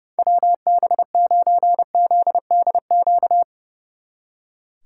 Morse code, heard as a keyed tone, about 38 wpm, 690 Hz. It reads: WB9ZDQ